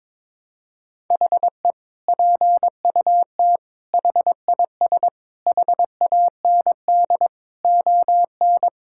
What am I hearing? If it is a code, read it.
Morse code, heard as a keyed tone, about 22 wpm, 700 Hz. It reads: HE PUT HIS HAND ON
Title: Sentences from Top 1000 Words - Review Part 6 - 22wpm